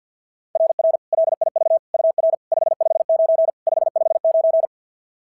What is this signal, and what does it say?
Morse code, heard as a keyed tone, about 45 wpm, 660 Hz. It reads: RR LIV UR 559 559
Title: SOTA QSO Practice 50wpm